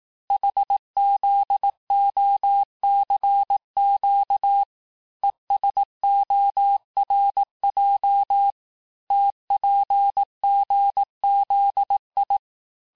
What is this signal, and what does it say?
Morse code, heard as a keyed tone, about 18 wpm, 800 Hz. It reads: HZOCQ ESORJ TPGZI
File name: cours-du-lundi files / cours-1852-texte-chiffre-vitesse-17MPM